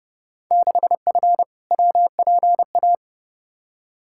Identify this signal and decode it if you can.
Morse code, heard as a keyed tone, about 30 wpm, 700 Hz. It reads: 6F WPA
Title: Field Day Exchanges Encore - 30wpm